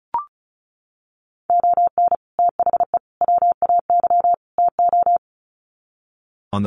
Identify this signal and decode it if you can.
Morse code, heard as a keyed tone, about 34 wpm, 700 Hz. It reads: ON THE WAY TO
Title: Sets of 4 Words 35wpm